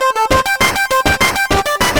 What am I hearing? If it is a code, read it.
There is no code.